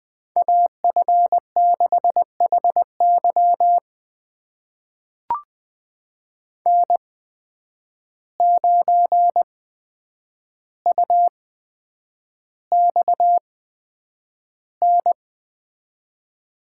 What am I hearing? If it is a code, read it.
Morse code, heard as a keyed tone, about 20 wpm, spaced as for 9 wpm, 700 Hz. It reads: AF6HY N9UXN
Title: USA Call Signs - 5 Characters - 8x Character Spacing - 20wpm